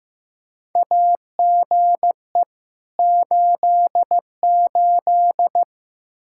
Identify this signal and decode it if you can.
Morse code, heard as a keyed tone, about 15 wpm, 700 Hz. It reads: AGE 88
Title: Practice QSO - Exchange 3 - More Info - 15wpm